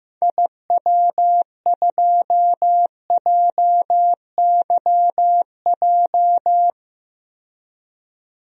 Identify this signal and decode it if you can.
Morse code, heard as a keyed tone, about 15 wpm, 700 Hz. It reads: IW2JYJ
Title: DXCC-100 Call Signs - 6 Characters - 15wpm